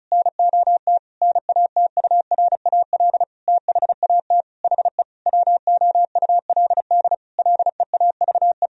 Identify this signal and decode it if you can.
Morse code, heard as a keyed tone, about 34 wpm, 700 Hz. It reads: NOT NATURAL THAT HE WOULD LEAVE
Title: Sentences from Top 1000 Words - Review Part 4 - 35wpm